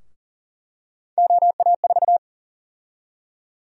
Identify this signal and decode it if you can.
Morse code, heard as a keyed tone, about 40 wpm, 700 Hz. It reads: OA4
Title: USA Call Sign Prefixes - Repeated 2x - 40wpm